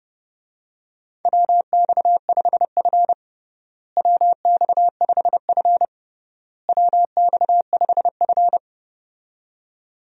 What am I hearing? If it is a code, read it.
Morse code, heard as a keyed tone, about 30 wpm, 700 Hz. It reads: WX5F WX5F WX5F
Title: USA Call Signs - 4 Characters - Repeated 3x - 30wpm